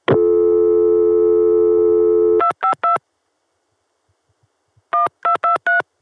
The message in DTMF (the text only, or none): 252 1223